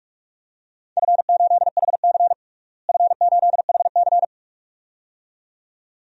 Morse code, heard as a keyed tone, about 41 wpm, 700 Hz. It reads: F8HC F8HC
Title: DXCC-100 Call Signs - 4 Characters - Repeated 2x - 45wpm